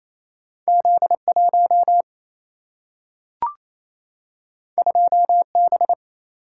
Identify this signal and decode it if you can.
Morse code, heard as a keyed tone, about 28 wpm, 700 Hz. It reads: Z 1 2 B
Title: Single-Letter Single-Number 28wpm